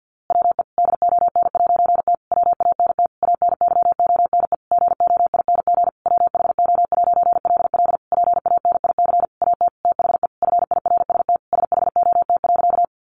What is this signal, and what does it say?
Morse code, heard as a keyed tone, about 50 wpm, 700 Hz. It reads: WE DON'T WANT ANYONE GOING WHO'LL PANIC AT THE FIRST SHOT.
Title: A Sound Of Thunder 50wpm